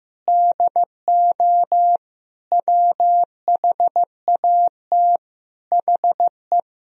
Morse code, heard as a keyed tone, about 15 wpm, 700 Hz. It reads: DO WHAT HE
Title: Sentences from Top 100 Words and Abbreviations 15wpm